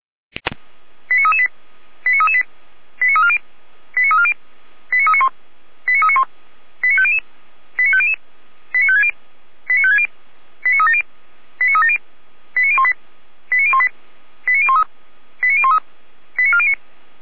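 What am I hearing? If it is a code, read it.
Five-tone selective call (ZVEI), heard as quick runs of five tones, tones of about 80 ms. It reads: 89338 89338 89340 89340 89391 89391 89500 89500 89560 89560 89370 89370 89018 89018 89013 89013 89409
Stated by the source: ZVEI five-tone call received by SDR, on an archive of audio.f85360.feuerwehr-orlamuende.org (Feuerwehr Orlamünde fire brigade).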